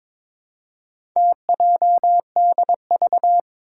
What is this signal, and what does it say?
Morse code, heard as a keyed tone, about 22 wpm, 700 Hz. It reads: TJDV
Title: Four Challenging Letter-Number Combinations 22wpm